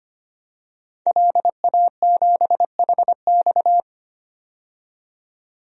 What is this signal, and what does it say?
Morse code, heard as a keyed tone, about 25 wpm, 700 Hz. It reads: LA7HX
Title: DXCC-100 Call Signs - 5 Characters - 25wpm